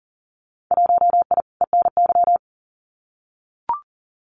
Morse code, heard as a keyed tone, about 40 wpm, 700 Hz. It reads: 1I ENY
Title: Winter Field Day Exchanges Encore 40wpm